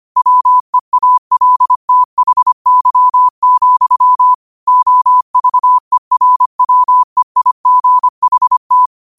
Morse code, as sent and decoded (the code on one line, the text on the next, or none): .-- . .- .-.. - .... -.-- --..-- / --- ...- . .-. .-- . .. --. .... -
WEALTHY, OVERWEIGHT